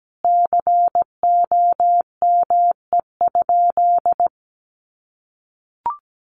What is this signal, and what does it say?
Morse code, heard as a keyed tone, about 17 wpm, 700 Hz. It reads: COME?
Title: Sentences from Top 200 Words 17wpm